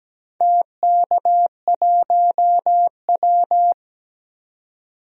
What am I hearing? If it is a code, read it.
Morse code, heard as a keyed tone, about 17 wpm, 700 Hz. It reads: TK1W
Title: DXCC-100 Call Signs - 4 Characters - 17wpm